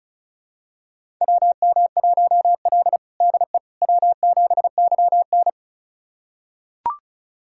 Morse code, heard as a keyed tone, about 36 wpm, 700 Hz. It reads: WM1L DE W7YN
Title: USA Callsign DE Callsign - 4 Characters - 35wpm